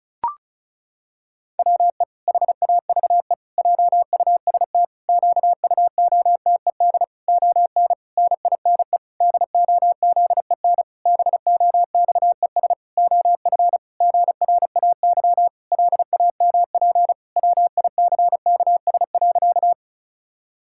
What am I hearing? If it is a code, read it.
Morse code, heard as a keyed tone, about 36 wpm, 700 Hz. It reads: WE HAVE JUST QUOTED ON NINE DOZEN BOXES OF GRAY LAMP WICKS.
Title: Pangram Challenge v2 - 35wpm